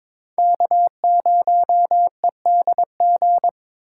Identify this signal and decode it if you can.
Morse code, heard as a keyed tone, about 26 wpm, 700 Hz. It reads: K0EDG
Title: Speed Racing - USA 1x3 Call Signs 22wpm